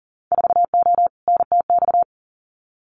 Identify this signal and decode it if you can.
Morse code, heard as a keyed tone, about 40 wpm, 700 Hz. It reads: 4O NTX
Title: Winter Field Day Exchanges 40wpm